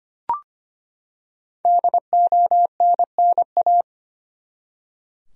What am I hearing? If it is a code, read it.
Morse code, heard as a keyed tone, about 25 wpm, 700 Hz. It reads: DONNA
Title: USA Common Names - 25wpm